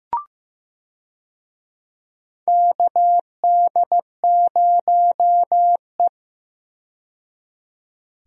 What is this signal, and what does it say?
Morse code, heard as a keyed tone, about 15 wpm, 700 Hz. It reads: KD0E
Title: USA Call Signs - 4 Characters - 15wpm